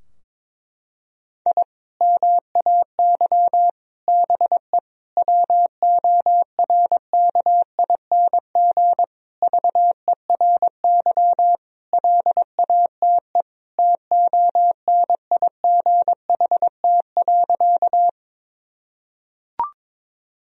Morse code, heard as a keyed tone, about 22 wpm, 700 Hz. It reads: I MAY BE WORKING VERY LATE TONIGHT.
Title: Sentences from Top 1000 Words - Review Part 4 - 22wpm